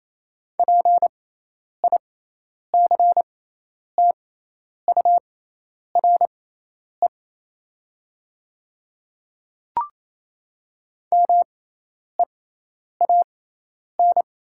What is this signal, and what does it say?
Morse code, heard as a keyed tone, about 28 wpm, spaced as for 10 wpm, 700 Hz. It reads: PICTURE MEAN
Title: 6x Character Spacing - Top 200 Words - 28wpm